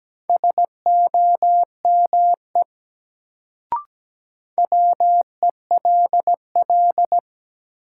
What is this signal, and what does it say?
Morse code, heard as a keyed tone, about 17 wpm, 700 Hz. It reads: SOME WELL
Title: Sets of 2 Words - New Word Drill - 17wpm